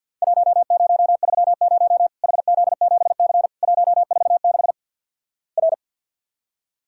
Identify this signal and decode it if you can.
Morse code, heard as a keyed tone, about 50 wpm, 700 Hz. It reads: 1030 HZ7C 146 R
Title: Worked All Europe DX Contest - QTC Exchange with Repeats - 50wpm